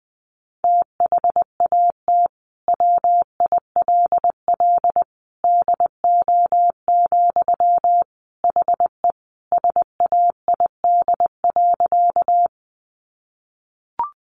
Morse code, heard as a keyed tone, about 20 wpm, 700 Hz. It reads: THAT WILL DO, HE SAID.
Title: Sentences from Top 100 Words 20wpm